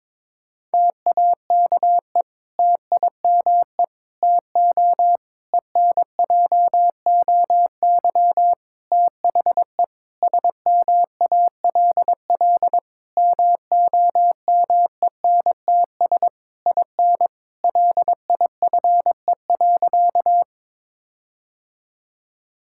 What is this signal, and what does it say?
Morse code, heard as a keyed tone, about 22 wpm, 700 Hz. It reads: TAKE TIME TO ENJOY THE SMALL MOMENTS IN LIFE.
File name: Difficult Sentences with Hints - 22wpm